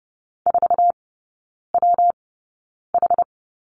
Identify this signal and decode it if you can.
Morse code, heard as a keyed tone, about 30 wpm, spaced as for 9 wpm, 700 Hz. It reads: 4WH
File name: Lesson 19e - Letter w - Three Characters 30wpm